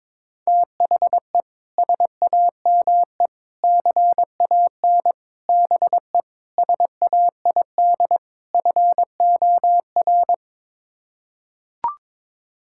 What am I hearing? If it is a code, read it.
Morse code, heard as a keyed tone, about 22 wpm, 700 Hz. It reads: THE SAME CAN BE SAID FOR
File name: practice / 22wpm / Sets of 6 Words - 22wpm